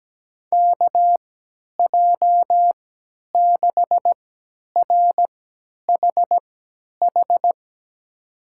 Morse code, heard as a keyed tone, about 17 wpm, spaced as for 10 wpm, 700 Hz. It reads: KJ6RHH
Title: USA Call Signs - 6 Characters - 3x Character Spacing - 17wpm